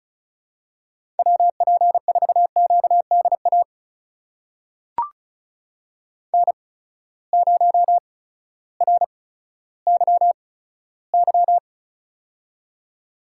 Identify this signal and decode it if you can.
Morse code, heard as a keyed tone, about 36 wpm, spaced as for 16 wpm, 700 Hz. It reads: WP4QDA N0RYY
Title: USA Call Signs - 8x Character Spacing - 35wpm